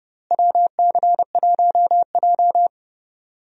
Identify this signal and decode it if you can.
Morse code, heard as a keyed tone, about 30 wpm, 700 Hz. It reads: WC1J
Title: USA Call Signs - 4 Characters - Repeated 3x - 30wpm